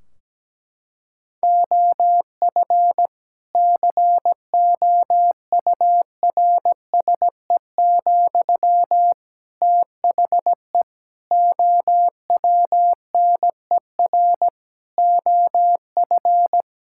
Morse code, heard as a keyed tone, about 17 wpm, 700 Hz. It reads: OF COURSE, THE OWNER OF
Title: Sentences from Top 1600 Words 17wpm